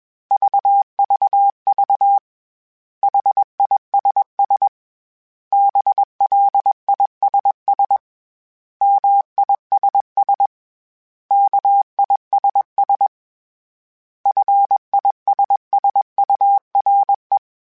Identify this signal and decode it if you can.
Morse code, heard as a keyed tone, about 21 wpm, 800 Hz. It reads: VVV HISS BLISS MISS KISS FISSURE
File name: Special INGs / Special INGs-mp3s / Suffixes @20